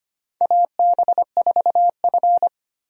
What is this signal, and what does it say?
Morse code, heard as a keyed tone, about 25 wpm, 700 Hz. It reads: AB4F
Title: USA Call Signs - 4 Characters - 25wpm